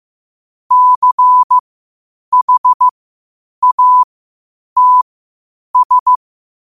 Morse code, heard as a keyed tone, about 15 wpm, spaced as for 8 wpm, 1000 Hz. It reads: CHATS
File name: qotdfiles / 8wpm